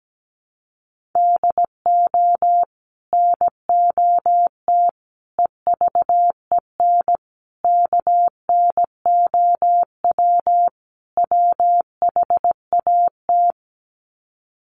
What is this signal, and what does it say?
Morse code, heard as a keyed tone, about 17 wpm, 700 Hz. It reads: DO NOT EVEN KNOW WHAT
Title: Sets of 5 Words Encore 17wpm